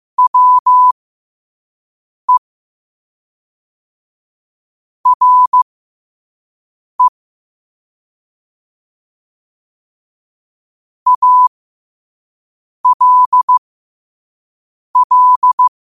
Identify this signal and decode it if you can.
Morse code, heard as a keyed tone, about 15 wpm, spaced as for 5 wpm, 1000 Hz. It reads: WE RE ALL
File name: qotdfiles / 5wpm